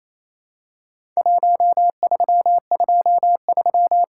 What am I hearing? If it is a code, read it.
Morse code, heard as a keyed tone, about 28 wpm, 700 Hz. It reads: 1323
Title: Worked All Europe DX Contest - QTC Exchange - 28wpm